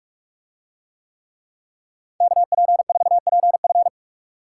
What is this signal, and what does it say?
Morse code, heard as a keyed tone, about 44 wpm, 700 Hz. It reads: KP4PF